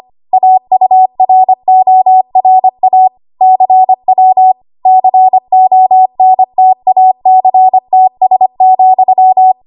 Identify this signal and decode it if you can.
Morse code, heard as a keyed tone, about 25 wpm, 750 Hz. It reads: AURORA CW CONTACTS,
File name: Morse / 25WPM